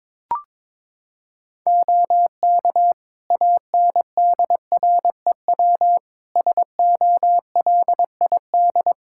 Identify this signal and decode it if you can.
Morse code, heard as a keyed tone, about 22 wpm, 700 Hz. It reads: OK ANDREW SOLID
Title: Practice QSO - Exchange 3 - More Info - 22wpm